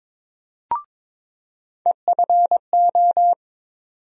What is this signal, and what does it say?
Morse code, heard as a keyed tone, about 21 wpm, 700 Hz. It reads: EFO
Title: USA Call Sign Postfixes - 3 Letters 22wpm